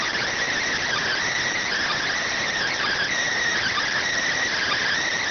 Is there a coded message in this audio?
Yes: Emergency Alert System (SAME) tones from 0.8 to 4.9 s, heard as harsh buzzing bursts.